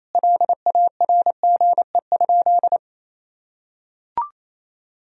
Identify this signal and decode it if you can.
Morse code, heard as a keyed tone, about 28 wpm, 700 Hz. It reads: LARGE?
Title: Sentences from Top 1000 Words - Review Part 1 - 28wpm